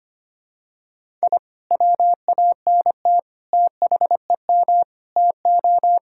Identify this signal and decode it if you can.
Morse code, heard as a keyed tone, about 25 wpm, 700 Hz. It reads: I WANT THEM TO
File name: practice / 25wpm / Sentences with 2-Word Phrases - 25wpm